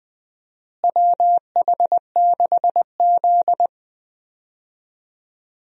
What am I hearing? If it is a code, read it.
Morse code, heard as a keyed tone, about 20 wpm, 700 Hz. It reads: WH6Z
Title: USA Call Signs - 4 Characters - Repeated 2x - 20wpm